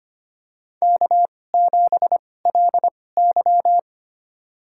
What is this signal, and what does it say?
Morse code, heard as a keyed tone, about 25 wpm, spaced as for 18 wpm, 700 Hz. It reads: K7LY
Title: USA Call Signs - 4 Characters - 2x Character Spacing - 25wpm